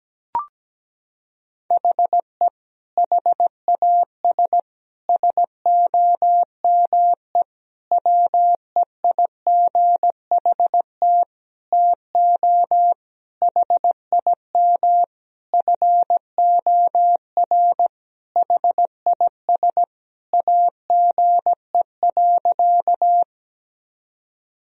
Morse code, heard as a keyed tone, about 17 wpm, 700 Hz. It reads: HE HAS SOME WEIGHT TO HIM FOR HIS AGE.
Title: Sentences from Top 1000 Words - Review Part 3 - 17wpm